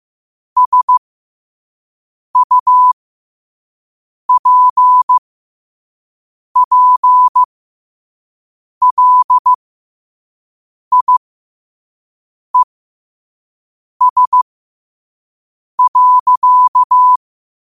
Morse code, heard as a keyed tone, about 15 wpm, spaced as for 5 wpm, 1000 Hz. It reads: SUPPLIES.
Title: 5 WPM morse code quotes for Fri, 01 Mar 2024